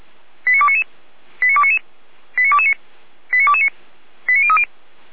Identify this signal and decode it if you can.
Five-tone selective call (ZVEI), heard as quick runs of five tones, tones of about 70 ms. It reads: 89299 89299 89229 89229 89030